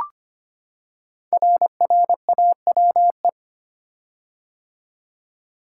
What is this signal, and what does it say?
Morse code, heard as a keyed tone, about 25 wpm, 700 Hz. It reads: RRAWE
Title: Code Groups - 5 Letters - 25wpm